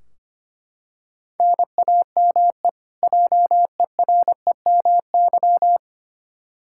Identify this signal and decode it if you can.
Morse code, heard as a keyed tone, about 25 wpm, 700 Hz. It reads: NAME JEREMY
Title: K1USN SST QSO Practice 25wpm